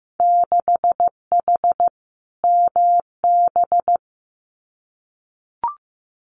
Morse code, heard as a keyed tone, about 15 wpm, 700 Hz. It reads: 6H MB
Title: Winter Field Day Exchanges Encore 15wpm